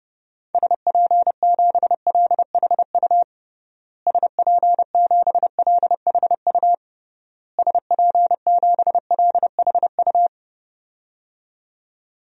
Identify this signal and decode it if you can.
Morse code, heard as a keyed tone, about 30 wpm, 700 Hz. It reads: SP7LHU SP7LHU SP7LHU
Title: DXCC-100 Call Signs - 6 Characters - Repeated 3x - 30wpm